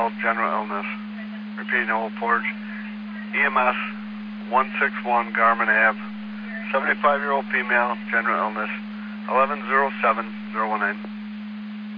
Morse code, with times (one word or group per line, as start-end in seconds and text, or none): none